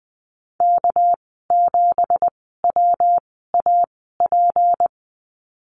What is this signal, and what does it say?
Morse code, heard as a keyed tone, about 20 wpm, spaced as for 15 wpm, 700 Hz. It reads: K7WAP
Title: 213 Lesson 34g - Number 8 - Call Signs 20wpm